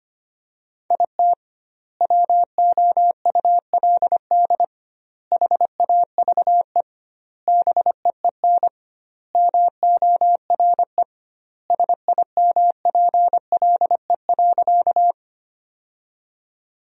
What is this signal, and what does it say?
Morse code, heard as a keyed tone, about 25 wpm, 700 Hz. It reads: IT WOULD HAVE BEEN MORE SIMPLE.
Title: 2x Word-Spacing - Sentences from Top 400 Words - 25wpm